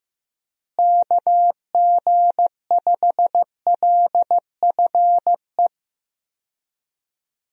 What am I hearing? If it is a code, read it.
Morse code, heard as a keyed tone, about 15 wpm, 700 Hz. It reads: KG5LFE
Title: USA Call Signs - 6 Characters - Repeated 3x - 15wpm